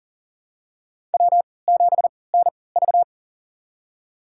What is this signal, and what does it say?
Morse code, heard as a keyed tone, about 40 wpm, spaced as for 23 wpm, 700 Hz. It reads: W7NV
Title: USA Call Signs - 3x Character Spacing - 40wpm